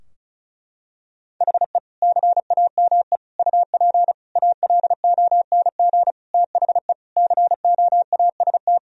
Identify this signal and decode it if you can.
Morse code, heard as a keyed tone, about 36 wpm, 700 Hz. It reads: HE CAME UP ALONG THE COAST
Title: Sentences from Top 1000 Words - Review Part 4 - 35wpm